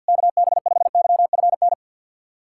Morse code, heard as a keyed tone, about 52 wpm, 700 Hz. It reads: KB5YFN
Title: SOTA QSO Practice 50wpm